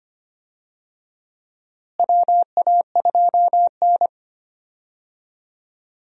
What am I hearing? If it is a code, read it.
Morse code, heard as a keyed tone, about 25 wpm, 700 Hz. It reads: WA2N
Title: USA Call Signs - 4 Characters - Repeated 2x - 25wpm